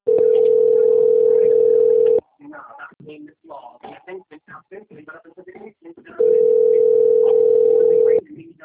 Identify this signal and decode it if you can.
Ringback tone (North America), 440 + 480 Hz, with speech.
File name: out- - - - - .59641